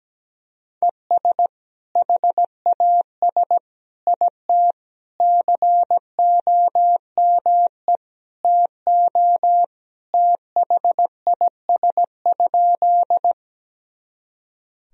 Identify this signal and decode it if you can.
Morse code, heard as a keyed tone, about 17 wpm, 700 Hz. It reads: ES HAS IT COME TO THIS?
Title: Sentences from Top 100 Words and Abbreviations 17wpm